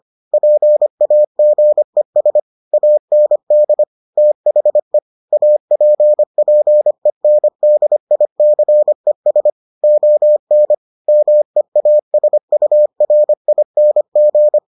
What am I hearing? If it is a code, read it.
Morse code, heard as a keyed tone, about 25 wpm, 600 Hz. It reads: PAGES AND THE APPENDICES ON MEASURING